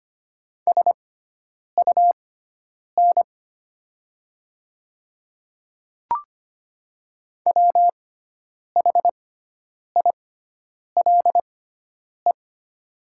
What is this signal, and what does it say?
Morse code, heard as a keyed tone, about 25 wpm, spaced as for 9 wpm, 700 Hz. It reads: SUN WHILE
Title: 6x Character Spacing - Top 300 Words - 25wpm